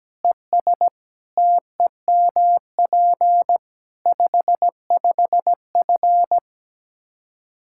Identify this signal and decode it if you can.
Morse code, heard as a keyed tone, about 17 wpm, 700 Hz. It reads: ES TEMP 55F
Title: Practice QSO - Exchange 2 - Info - 17wpm